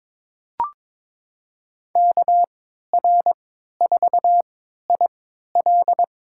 Morse code, heard as a keyed tone, about 22 wpm, spaced as for 13 wpm, 700 Hz. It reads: KR4IL
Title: USA Call Signs - 5 Characters - 3x Character Spacing - 22wpm